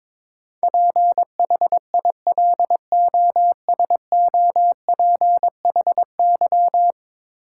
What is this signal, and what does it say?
Morse code, heard as a keyed tone, about 22 wpm, 700 Hz. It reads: PHILOSOPHY
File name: practice / 22wpm / 10 Letter Words - First 3 Letters Given - 22wpm